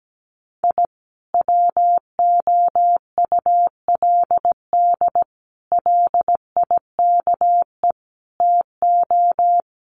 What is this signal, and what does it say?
Morse code, heard as a keyed tone, about 17 wpm, 700 Hz. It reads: I WOULD LIKE TO
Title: Sets of 4 Words 17wpm